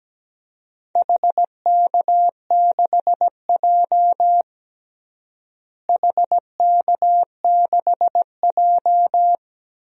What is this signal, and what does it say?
Morse code, heard as a keyed tone, about 17 wpm, 700 Hz. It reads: HK6J HK6J
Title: DXCC-100 Call Signs - 4 Characters - Repeated 3x - 17wpm